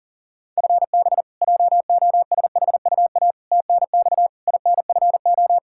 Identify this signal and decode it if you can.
Morse code, heard as a keyed tone, about 40 wpm, 700 Hz. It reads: FB JOSHUA TNX INFO
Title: K1USN SST QSO Practice 40wpm